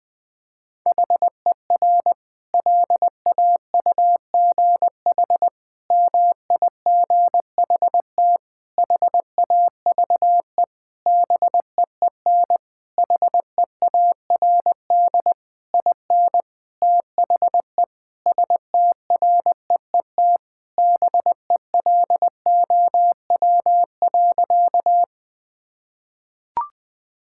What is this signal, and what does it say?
Morse code, heard as a keyed tone, about 21 wpm, 700 Hz. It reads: HER LAUGH MIGHT HAVE BEEN HEARD IN THE STREET BELOW.
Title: Sentences from Top 1000 Words - Review Part 5 - 20wpm